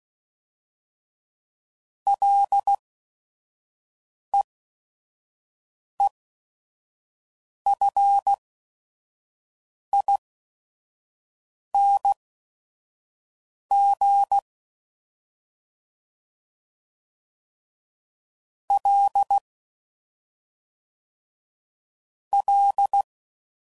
Morse code, heard as a keyed tone, about 16 wpm, spaced as for 5 wpm, 800 Hz. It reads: LEEFING L L
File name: Prefixes / FLs & THs / Single Fs Ls